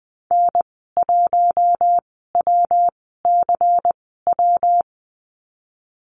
Morse code, heard as a keyed tone, about 20 wpm, spaced as for 15 wpm, 700 Hz. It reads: N1WCW
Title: 128 Lesson 21g - Number 6 - Call Signs 20wpm